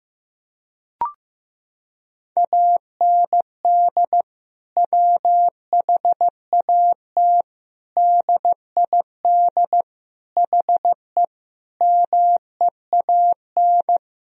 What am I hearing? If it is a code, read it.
Morse code, heard as a keyed tone, about 15 wpm, 700 Hz. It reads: AND WHAT DID HE MEAN
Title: Sentences from Top 1000 Words - Review Part 3 - 15wpm